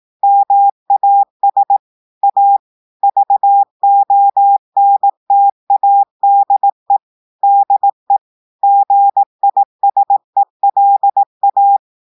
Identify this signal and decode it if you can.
Morse code, heard as a keyed tone, about 18 wpm, 800 Hz. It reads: MAS A VONTADE DE GISELA
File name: audios / portugais / portugais texte en clair 1 vitesse 18 mots minutes